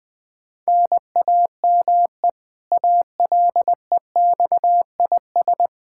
Morse code, heard as a keyed tone, about 20 wpm, 700 Hz. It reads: NAME ALEXIS
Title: Practice QSO - Exchange 1 - Beginning - 20wpm